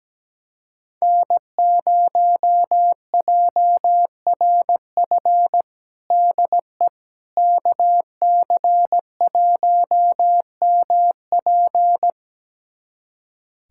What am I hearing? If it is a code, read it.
Morse code, heard as a keyed tone, about 17 wpm, 700 Hz. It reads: N0JRF DE KC1MP
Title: USA Callsign DE Callsign - 5 Characters - 17wpm